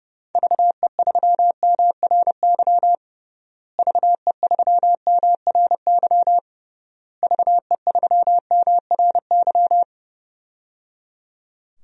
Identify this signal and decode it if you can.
Morse code, heard as a keyed tone, about 30 wpm, 700 Hz. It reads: VE3MRY VE3MRY VE3MRY
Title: DXCC-100 Call Signs - 6 Characters - Repeated 3x - 30wpm